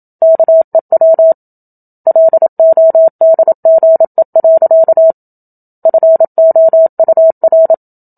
Morse code, heard as a keyed tone, about 27 wpm, 650 Hz. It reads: KEW LODGE. FOUR
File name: Book Two / 01-Chapter 1 - 27 WPM